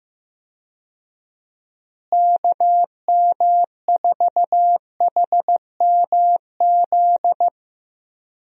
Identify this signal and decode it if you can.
Morse code, heard as a keyed tone, about 15 wpm, 700 Hz. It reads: KM4HMZ
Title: USA Call Signs - 6 Characters - Repeated 3x - 15wpm